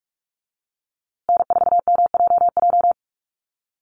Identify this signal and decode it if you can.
Morse code, heard as a keyed tone, about 44 wpm, 700 Hz. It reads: N4MJJ
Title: USA Call Signs Repeated 3x 45wpm